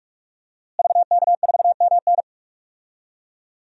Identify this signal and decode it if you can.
Morse code, heard as a keyed tone, about 47 wpm, 700 Hz. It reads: VK4MN
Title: DXCC-100 Call Signs - 5 Characters - Repeated 2x - 45wpm